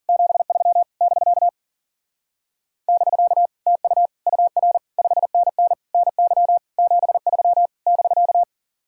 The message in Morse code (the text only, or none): 73 <BK> <BK> TU UR 5NN NY 73 <BK>